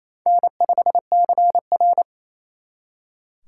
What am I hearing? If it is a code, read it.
Morse code, heard as a keyed tone, about 34 wpm, 700 Hz. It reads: N5CR
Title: Speed Racing - USA 1x2 Call Signs 28wpm